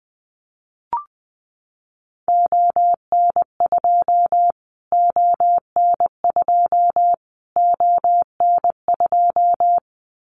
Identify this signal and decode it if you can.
Morse code, heard as a keyed tone, about 20 wpm, 700 Hz. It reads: ON2 ON2 ON2
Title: Callsign Prefixes for Top 25 DXCC - Repeated 3x - 20wpm